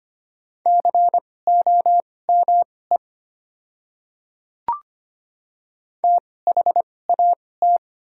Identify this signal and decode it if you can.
Morse code, heard as a keyed tone, about 25 wpm, spaced as for 18 wpm, 700 Hz. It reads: COME THAT
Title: 2x Character Spacing - Top 300 Words - 25wpm